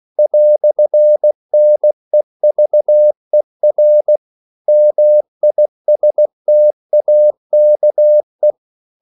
Morse code, heard as a keyed tone, about 16 wpm, 600 Hz. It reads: "NEVER MISTAKE
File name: CW / Practice / EnglishQuotations0416